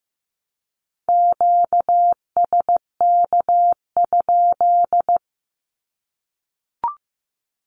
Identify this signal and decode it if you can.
Morse code, heard as a keyed tone, about 15 wpm, 700 Hz. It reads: QSK?
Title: Q-Codes 15wpm